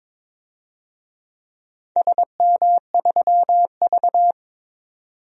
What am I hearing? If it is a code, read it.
Morse code, heard as a keyed tone, about 22 wpm, 700 Hz. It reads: SM3V
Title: DXCC-100 Call Signs - 4 Characters - 22wpm